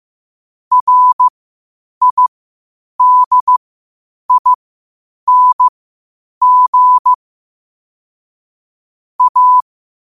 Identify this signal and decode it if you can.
Morse code, heard as a keyed tone, about 15 wpm, spaced as for 8 wpm, 1000 Hz. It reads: RIDING A
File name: qotdfiles / 8wpm